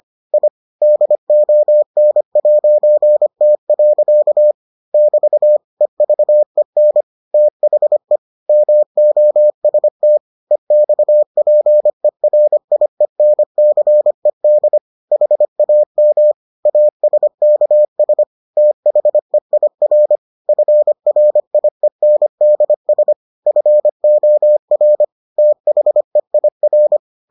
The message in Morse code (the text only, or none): I DON'T. <BT> EVEN THE MOST EXPERIENCED HAM ASKS THEIR FRIENDS FOR THEIR